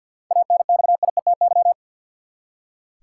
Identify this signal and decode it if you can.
Morse code, heard as a keyed tone, about 46 wpm, 700 Hz. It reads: ANXIETY
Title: Seven Letter Words - 50wpm